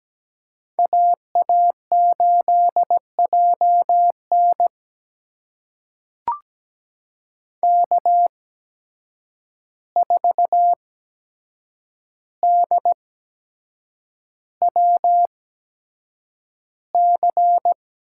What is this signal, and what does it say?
Morse code, heard as a keyed tone, about 17 wpm, spaced as for 7 wpm, 700 Hz. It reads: AA8JN K4DWC
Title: USA Call Signs - 5 Characters - 8x Character Spacing - 17wpm